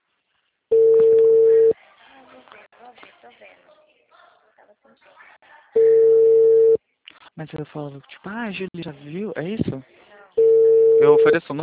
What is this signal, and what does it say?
Ringback tone (China, 450 Hz), with speech.